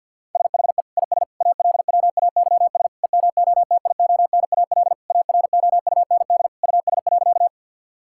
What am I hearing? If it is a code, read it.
Morse code, heard as a keyed tone, about 52 wpm, 690 Hz. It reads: SHE IS ALWAYS EMOTIONAL AROUND US.
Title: Sentences from Top 1000 Words - Review Part 5 - 50wpm